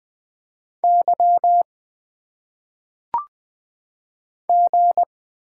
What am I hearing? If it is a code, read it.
Morse code, heard as a keyed tone, about 20 wpm, 700 Hz. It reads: Y G
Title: Single Letters - Phonetic Pronunciation 20wpm